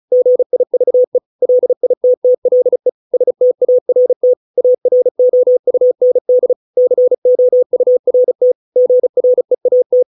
Morse code, heard as a keyed tone, about 35 wpm, 500 Hz. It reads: GIVE LITTLE START AROUND COURT GREAT